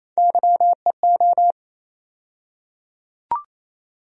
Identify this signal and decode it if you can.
Morse code, heard as a keyed tone, about 27 wpm, 700 Hz. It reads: YEO
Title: USA Call Sign Postfixes - 3 Letters 28wpm